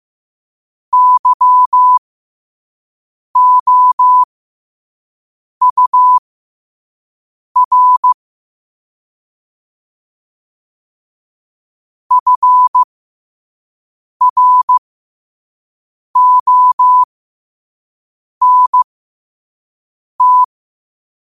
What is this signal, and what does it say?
Morse code, heard as a keyed tone, about 15 wpm, spaced as for 5 wpm, 1000 Hz. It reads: YOUR FRONT
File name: qotdfiles / 5wpm